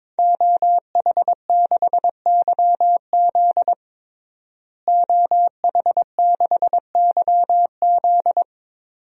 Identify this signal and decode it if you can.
Morse code, heard as a keyed tone, about 21 wpm, 700 Hz. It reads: OH6YZ OH6YZ
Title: DXCC-100 Call Signs - 5 Characters - Repeated 2x - 22wpm